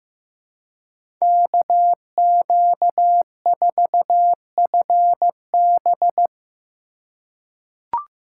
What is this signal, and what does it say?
Morse code, heard as a keyed tone, about 15 wpm, 700 Hz. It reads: KQ4FB
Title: USA Call Signs - 5 Characters - 3x Character Spacing - 15wpm